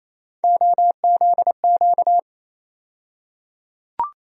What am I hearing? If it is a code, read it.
Morse code, heard as a keyed tone, about 28 wpm, 700 Hz. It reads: OZQ
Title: USA Call Sign Postfixes - 3 Letters 28wpm